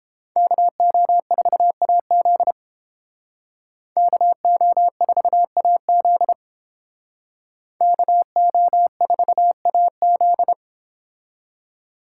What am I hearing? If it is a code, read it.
Morse code, heard as a keyed tone, about 30 wpm, 700 Hz. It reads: KO4AZ KO4AZ KO4AZ